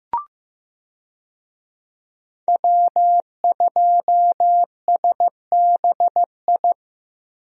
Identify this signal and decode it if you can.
Morse code, heard as a keyed tone, about 15 wpm, 700 Hz. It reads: W2SBI
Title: USA Call Signs - 5 Characters - Repeated 3x - 15wpm